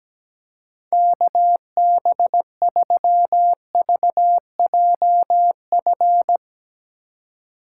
Morse code, heard as a keyed tone, about 18 wpm, 700 Hz. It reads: KB3VJF
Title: Speed Racing - USA 2x3 Call Signs 17wpm